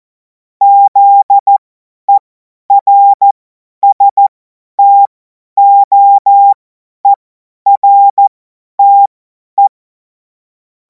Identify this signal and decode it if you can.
Morse code, heard as a keyed tone, about 14 wpm, spaced as for 10 wpm, 800 Hz. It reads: ZERSTOERTE
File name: Morsekurs / Witze / Witz 05. Mai 2026 50 F70